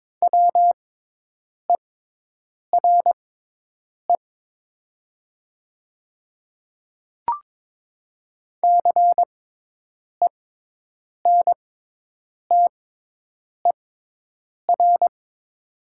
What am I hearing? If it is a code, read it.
Morse code, heard as a keyed tone, about 22 wpm, spaced as for 8 wpm, 700 Hz. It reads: WERE CENTER
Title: 6x Character Spacing - Top 500 Words - 22wpm